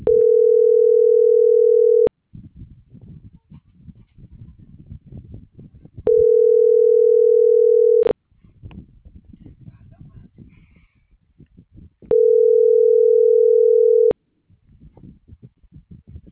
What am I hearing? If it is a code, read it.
Ringback tone (North America), 440 + 480 Hz.